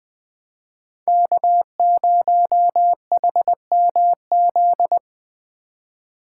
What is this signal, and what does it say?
Morse code, heard as a keyed tone, about 21 wpm, 700 Hz. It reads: K0HMZ
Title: K1USN SST QSO Practice - 20wpm